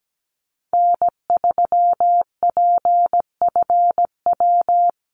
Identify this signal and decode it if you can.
Morse code, heard as a keyed tone, about 17 wpm, 700 Hz. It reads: N3PFW
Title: USA Call Signs Repeated 3x 17wpm